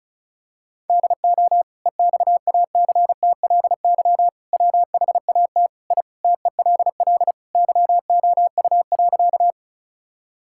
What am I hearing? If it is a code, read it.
Morse code, heard as a keyed tone, about 34 wpm, 700 Hz. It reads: DO EXACTLY WHAT I TELL YOU.
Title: Sentences from Top 1000 Words - Review Part 6 - 35wpm